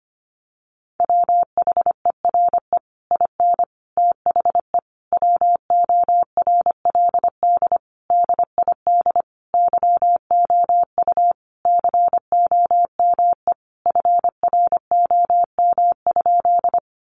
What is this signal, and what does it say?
Morse code, heard as a keyed tone, about 25 wpm, 700 Hz. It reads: WHERE IN THE WORLD DID YOU COME FROM?
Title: Sentences from Top 200 Words 25wpm